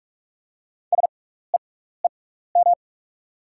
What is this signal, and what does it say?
Morse code, heard as a keyed tone, about 44 wpm, spaced as for 15 wpm, 700 Hz. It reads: SEEM
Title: 6x Character Spacing - Top 500 Words - 45wpm